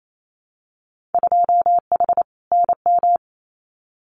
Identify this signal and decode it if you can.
Morse code, heard as a keyed tone, about 28 wpm, 690 Hz. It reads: 2H NM
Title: Winter Field Day Exchanges Encore 28wpm